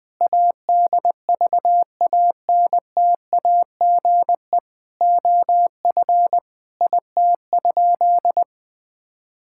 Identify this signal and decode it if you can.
Morse code, heard as a keyed tone, about 20 wpm, 700 Hz. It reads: ADVANTAGE OF IT?
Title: Sentences from Top 1000 Words - Review Part 6 - 20wpm